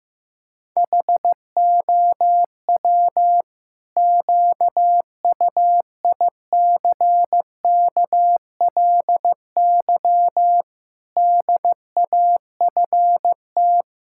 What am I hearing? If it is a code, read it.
Morse code, heard as a keyed tone, about 15 wpm, 700 Hz. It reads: HOW QUICKLY DAFT
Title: Pangram Challenge v2 - 15wpm